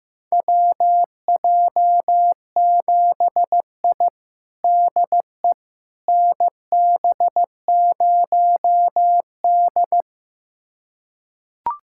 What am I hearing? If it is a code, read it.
Morse code, heard as a keyed tone, about 15 wpm, 700 Hz. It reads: WJ7I DE NB0D
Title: USA Callsign DE Callsign - 4 Characters - 15wpm